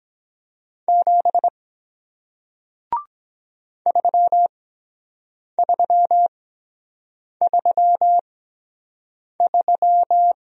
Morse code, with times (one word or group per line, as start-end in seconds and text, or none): 0.9-1.5 7
3.9-10.3 3333